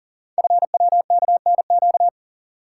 Morse code, heard as a keyed tone, about 42 wpm, 700 Hz. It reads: FWKNQ
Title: Code Groups - 5 Letters - 40wpm